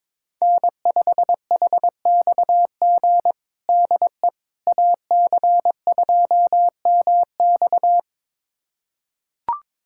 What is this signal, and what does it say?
Morse code, heard as a keyed tone, about 22 wpm, 700 Hz. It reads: N5HXG DE AC2MX
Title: USA Callsign DE Callsign - 5 Characters - 22wpm